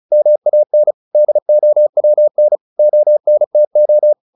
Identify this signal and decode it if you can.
Morse code, heard as a keyed tone, about 34 wpm, 600 Hz. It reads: MAN DOWN ONTO
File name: the old man and the sea / the old man and the sea 1 35wpm